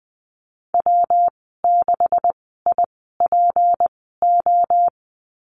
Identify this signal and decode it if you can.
Morse code, heard as a keyed tone, about 20 wpm, spaced as for 14 wpm, 700 Hz. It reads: W6IPO
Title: 167 Lesson 27g - Number 7 - Call Signs 20wpm